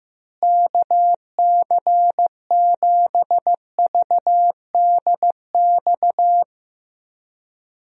Morse code, heard as a keyed tone, about 15 wpm, 700 Hz. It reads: KC7VDX
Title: USA Call Signs - 6 Characters - Repeated 3x - 15wpm